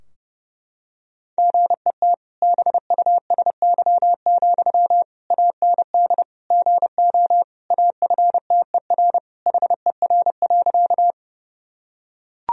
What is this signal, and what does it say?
Morse code, heard as a keyed tone, about 32 wpm, 700 Hz. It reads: GET BUSY, AND GO AFTER HER.
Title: Sentences from Top 1000 Words - Review Part 3 - 30wpm